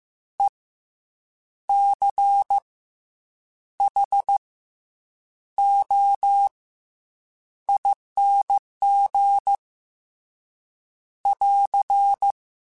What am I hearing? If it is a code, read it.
Morse code, heard as a keyed tone, about 15 wpm, 800 Hz. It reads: E C H O ING <AR>